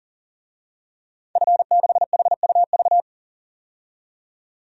Morse code, heard as a keyed tone, about 40 wpm, 700 Hz. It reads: F6HUV
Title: DXCC-100 Call Signs - 5 Characters - 40wpm